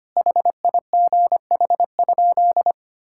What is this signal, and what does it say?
Morse code, heard as a keyed tone, about 25 wpm, 700 Hz. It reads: HIGH?
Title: Sentences from Top 1000 Words - Review Part 5 - 25wpm